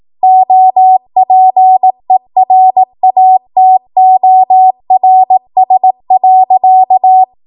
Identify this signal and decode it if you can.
Morse code, heard as a keyed tone, about 18 wpm, 750 Hz. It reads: OPERATORS.
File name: Morse / 18WPM